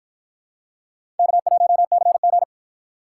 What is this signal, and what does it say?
Morse code, heard as a keyed tone, about 47 wpm, 700 Hz. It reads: K1CG